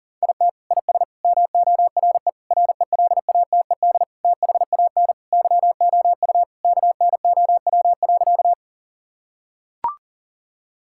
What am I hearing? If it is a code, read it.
Morse code, heard as a keyed tone, about 37 wpm, 700 Hz. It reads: IT IS MORE RELATED THAN YOU KNOW.
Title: Sentences from Top 1000 Words - Review Part 5 - 40wpm